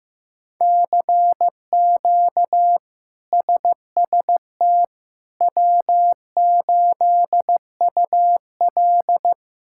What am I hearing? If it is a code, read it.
Morse code, heard as a keyed tone, about 16 wpm, 700 Hz. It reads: CQ SST W8UL
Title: K1USN SST QSO Practice - 15wpm